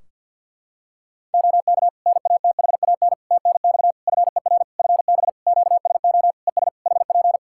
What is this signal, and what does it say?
Morse code, heard as a keyed tone, about 52 wpm, 700 Hz. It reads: OK NATHAN TNX FER FB QSO ES HP
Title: Practice QSO - Exchange 4 - Ending - 50wpm